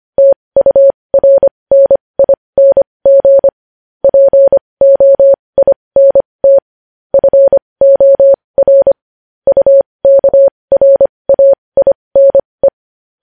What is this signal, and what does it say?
Morse code, heard as a keyed tone, about 25 wpm, spaced as for 20 wpm, 550 Hz. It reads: TURNING POINT FOR UKRAINE
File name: podcast / News headlines-1x-25x20